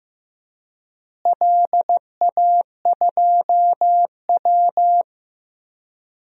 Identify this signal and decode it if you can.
Morse code, heard as a keyed tone, about 15 wpm, 700 Hz. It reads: LA2W
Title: DXCC-100 Call Signs - 4 Characters - Repeated 3x - 15wpm